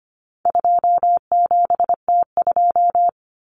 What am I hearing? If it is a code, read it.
Morse code, heard as a keyed tone, about 25 wpm, 700 Hz. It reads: 27T2
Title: Common Cut Numbers - 4-Digit 25wpm